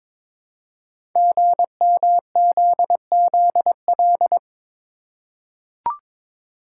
Morse code, heard as a keyed tone, about 22 wpm, 700 Hz. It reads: GMZZL
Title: Code Groups - 5 Letters - 22wpm